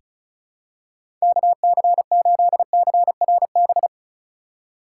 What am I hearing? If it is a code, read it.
Morse code, heard as a keyed tone, about 34 wpm, 700 Hz. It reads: KC8CRB
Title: USA Call Signs - 6 Characters - 4x Character Spacing - 35wpm